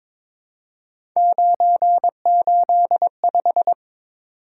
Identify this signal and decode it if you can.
Morse code, heard as a keyed tone, about 22 wpm, 700 Hz. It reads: 985
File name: practice / 22wpm / Three Letter-Number Combinations - 22wpm